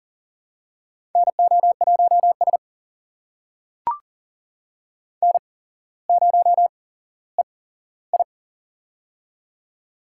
Morse code, heard as a keyed tone, about 39 wpm, spaced as for 17 wpm, 700 Hz. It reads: NO1S N0EI
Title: USA Call Signs - 4 Characters - 8x Character Spacing - 40wpm